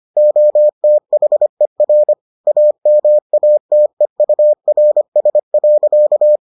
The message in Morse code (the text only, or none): OTHER AMATEURS.